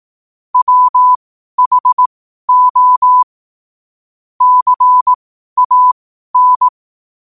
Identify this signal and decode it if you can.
Morse code, heard as a keyed tone, about 18 wpm, spaced as for 12 wpm, 1000 Hz. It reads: WHO CAN
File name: qotdfiles / 12wpm